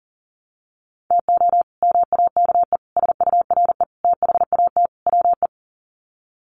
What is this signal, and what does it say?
Morse code, heard as a keyed tone, about 40 wpm, 700 Hz. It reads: TO MAKE SURE THAT WE
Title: Sets of 5 Words Encore 40wpm